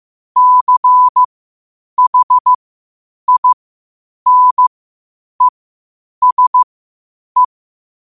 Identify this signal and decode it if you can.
Morse code, heard as a keyed tone, about 15 wpm, spaced as for 8 wpm, 1000 Hz. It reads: CHINESE
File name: qotdfiles / 8wpm